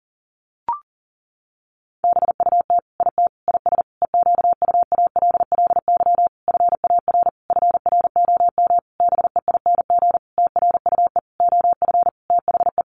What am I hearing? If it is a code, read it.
Morse code, heard as a keyed tone, about 40 wpm, 700 Hz. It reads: BUT IT IS EQUALLY FAR FROM BEING TRUE OF THE
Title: Sentences from Top 1400 Words 40wpm